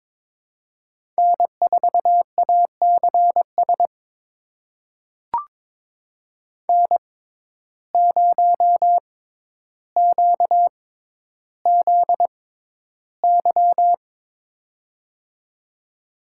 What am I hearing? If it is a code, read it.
Morse code, heard as a keyed tone, about 22 wpm, spaced as for 11 wpm, 700 Hz. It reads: N4ACS N0QZY
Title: USA Call Signs - 5 Characters - 6x Character Spacing - 22wpm